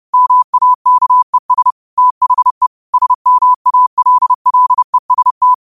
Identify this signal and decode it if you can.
Morse code, heard as a keyed tone, about 30 wpm, 1000 Hz. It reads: MAKES THE SMALLEST
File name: qotdfiles / 30wpm